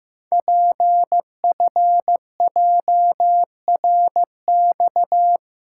Morse code, heard as a keyed tone, about 15 wpm, 700 Hz. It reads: PFJRX
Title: Code Groups - 5 Letters - 15wpm